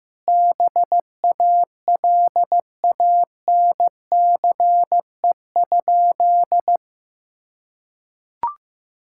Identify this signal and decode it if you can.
Morse code, heard as a keyed tone, about 16 wpm, 700 Hz. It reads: BALANCE?
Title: Sentences from Top 1000 Words - Review Part 6 - 15wpm